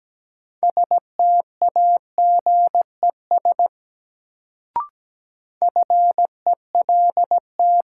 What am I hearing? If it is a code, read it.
Morse code, heard as a keyed tone, about 17 wpm, 700 Hz. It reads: STAGES FELT
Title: Sets of 5 Words Encore - New Word Drill - 17wpm